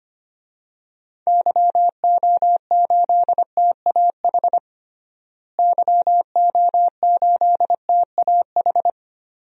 Morse code, heard as a keyed tone, about 25 wpm, 700 Hz. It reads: YO8TAH YO8TAH
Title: DXCC-100 Call Signs - 6 Characters - Repeated 2x - 25wpm